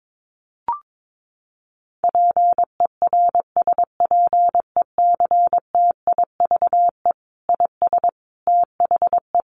Morse code, heard as a keyed tone, about 22 wpm, 700 Hz. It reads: PERSPECTIVE IS THE
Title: Sentences from Top 1200 Words 22wpm